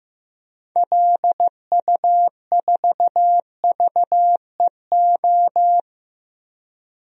Morse code, heard as a keyed tone, about 15 wpm, 700 Hz. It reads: LU4VEO
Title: DXCC-100 Call Signs - 6 Characters - Repeated 2x - 15wpm